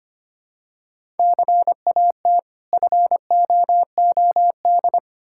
Morse code, heard as a keyed tone, about 25 wpm, 700 Hz. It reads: CAT FOOD
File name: practice / 25wpm / Difficult Words with Hints - 25wpm